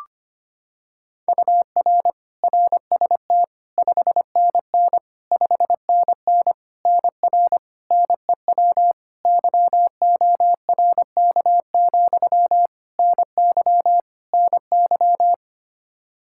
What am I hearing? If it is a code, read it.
Morse code, heard as a keyed tone, about 25 wpm, 700 Hz. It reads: UR RST 5NN 5NN NR NEW YORK, NY NY
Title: Practice QSO - Exchange 1 - Beginning - 25wpm